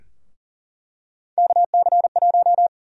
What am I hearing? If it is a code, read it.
Morse code, heard as a keyed tone, about 39 wpm, 700 Hz. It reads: KC1
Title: USA Call Sign Prefixes - Repeated 2x - 40wpm